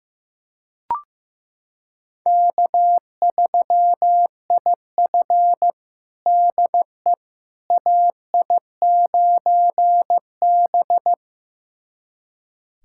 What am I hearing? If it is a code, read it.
Morse code, heard as a keyed tone, about 15 wpm, 700 Hz. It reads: K3IF DE AI9B
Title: USA Callsign DE Callsign - 4 Characters - 15wpm